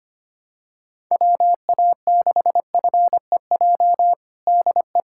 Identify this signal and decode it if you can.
Morse code, heard as a keyed tone, about 25 wpm, 700 Hz. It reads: WA6FEJ DE
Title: USA Callsign DE Callsign - 6 Characters - 25wpm